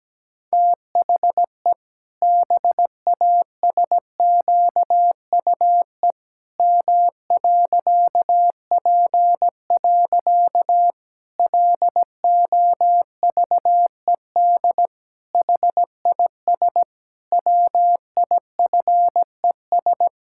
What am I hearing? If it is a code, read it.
Morse code, heard as a keyed tone, about 17 wpm, 700 Hz. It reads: THE BASQUE M.P. LOVED HIS WIFES
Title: Pangram Challenge v2 - 17wpm